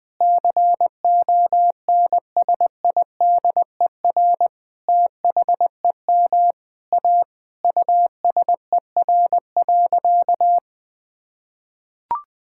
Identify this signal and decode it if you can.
Morse code, heard as a keyed tone, about 20 wpm, 700 Hz. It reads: CONSIDER THEM A USER.
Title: Sentences from Top 1000 Words - Review Part 5 - 20wpm